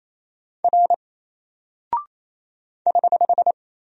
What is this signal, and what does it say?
Morse code, heard as a keyed tone, about 27 wpm, 700 Hz. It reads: R <HH>
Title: Single Character (Letter:Number:Punctuation:Prosign) 28wpm